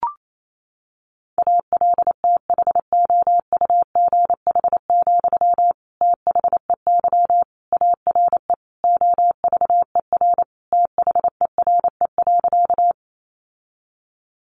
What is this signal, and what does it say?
Morse code, heard as a keyed tone, about 28 wpm, 700 Hz. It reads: ALTHOUGH, THEY ARE OVER THERE.
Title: Sentences from Top 600 Words 28wpm